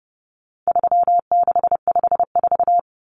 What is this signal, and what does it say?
Morse code, heard as a keyed tone, about 30 wpm, 700 Hz. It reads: 3654
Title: Common Cut Numbers - 4-Digit 30wpm